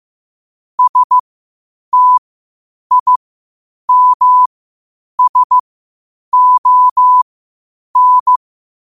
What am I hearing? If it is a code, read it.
Morse code, heard as a keyed tone, about 15 wpm, spaced as for 8 wpm, 1000 Hz. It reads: STIMSON
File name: qotdfiles / 8wpm